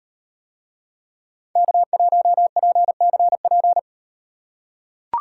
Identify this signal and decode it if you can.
Morse code, heard as a keyed tone, about 38 wpm, 700 Hz. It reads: K1PCP